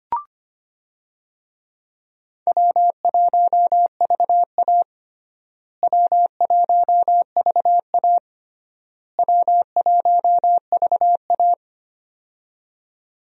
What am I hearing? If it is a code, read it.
Morse code, heard as a keyed tone, about 25 wpm, 700 Hz. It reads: W1VA W1VA W1VA
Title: USA Call Signs - 4 Characters - Repeated 3x - 25wpm